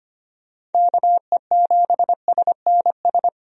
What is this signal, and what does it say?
Morse code, heard as a keyed tone, about 25 wpm, 700 Hz. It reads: KE7SNS
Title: USA Call Signs - 6 Characters - 25wpm